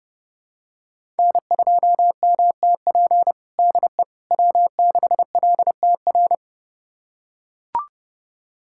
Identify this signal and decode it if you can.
Morse code, heard as a keyed tone, about 30 wpm, 700 Hz. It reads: N2MTP DE W6LTR
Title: USA Callsign DE Callsign - 5 Characters - 30wpm